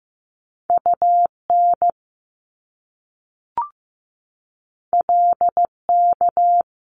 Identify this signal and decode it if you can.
Morse code, heard as a keyed tone, about 15 wpm, 700 Hz. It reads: UN LK
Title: Top Two Letter Combinations in Words 15wpm